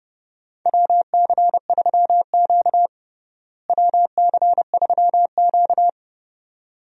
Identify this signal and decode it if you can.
Morse code, heard as a keyed tone, about 30 wpm, 700 Hz. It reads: WC3Q WC3Q
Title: USA Call Signs - 4 Characters - Repeated 2x - 30wpm